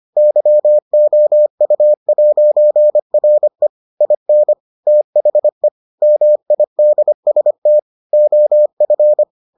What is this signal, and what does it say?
Morse code, heard as a keyed tone, about 25 wpm, 600 Hz. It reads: YOU'RE IN THE MIDST OF